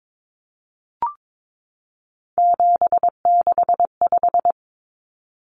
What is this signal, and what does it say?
Morse code, heard as a keyed tone, about 22 wpm, 700 Hz. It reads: 765
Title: Numbers - 3-Digit 22wpm